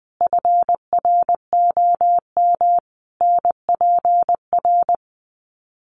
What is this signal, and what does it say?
Morse code, heard as a keyed tone, about 20 wpm, 700 Hz. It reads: FROM NPR
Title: Sets of 5 Words Encore 20wpm